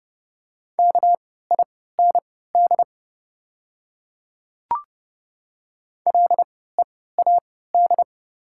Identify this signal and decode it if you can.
Morse code, heard as a keyed tone, about 30 wpm, spaced as for 17 wpm, 700 Hz. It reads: KIND LEAD
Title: 3x Character Spacing - Top 500 Words - 30wpm